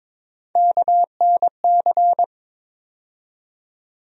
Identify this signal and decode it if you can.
Morse code, heard as a keyed tone, about 22 wpm, 700 Hz. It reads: KNC
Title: USA Call Sign Postfixes - 3 Letters 22wpm